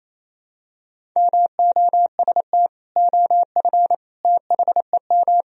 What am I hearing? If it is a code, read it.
Morse code, heard as a keyed tone, about 28 wpm, 700 Hz. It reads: MOST OF THEM
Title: Sentences from Top 1000 Words - Review Part 1 - 28wpm